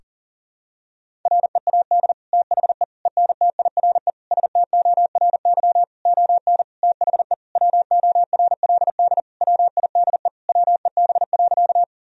Morse code, heard as a keyed tone, about 40 wpm, 700 Hz. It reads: READ THE ENTIRE STORY ON THE WORLD WIDE WEB.
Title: Sentences from Top 1000 Words - Review Part 4 - 40wpm